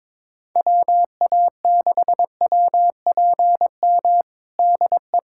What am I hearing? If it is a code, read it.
Morse code, heard as a keyed tone, about 22 wpm, 700 Hz. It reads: WA6WPM DE
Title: USA Callsign DE Callsign - 6 Characters - 22wpm